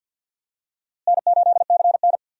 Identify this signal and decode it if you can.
Morse code, heard as a keyed tone, about 47 wpm, 700 Hz. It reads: N8CN
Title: USA Call Signs - 4 Characters - Repeated 2x - 50wpm